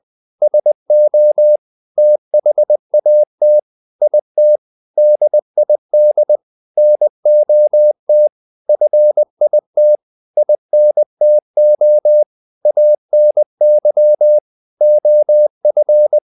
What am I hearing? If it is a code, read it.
Morse code, heard as a keyed tone, about 20 wpm, 600 Hz. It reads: SO THAT IT DID NOT FIT INTO ANY OF
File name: the old man and the sea part 2 20wpm